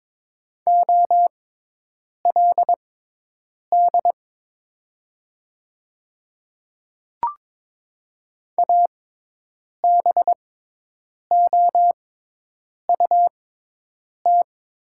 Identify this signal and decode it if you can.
Morse code, heard as a keyed tone, about 22 wpm, spaced as for 8 wpm, 700 Hz. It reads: OLD ABOUT
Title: 6x Character Spacing - Top 200 Words - 22wpm